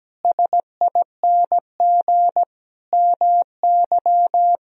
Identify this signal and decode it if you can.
Morse code, heard as a keyed tone, about 17 wpm, 700 Hz. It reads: SING MY
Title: Sentences from Top 1000 Words - Review Part 2 - 17wpm